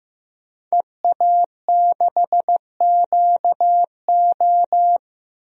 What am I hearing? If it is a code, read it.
Morse code, heard as a keyed tone, about 15 wpm, 700 Hz. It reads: EA6QO
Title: DXCC-100 Call Signs - 5 Characters - 15wpm